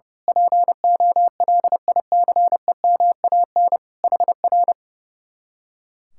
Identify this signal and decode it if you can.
Morse code, heard as a keyed tone, about 30 wpm, 700 Hz. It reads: POLICEMAN HR
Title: Speed-Racing Common QSO Phrases - 30wpm